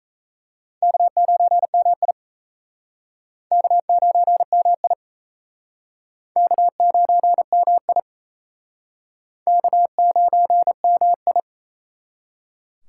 Morse code, heard as a keyed tone, about 37 wpm, 700 Hz. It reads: K9MI K9MI K9MI K9MI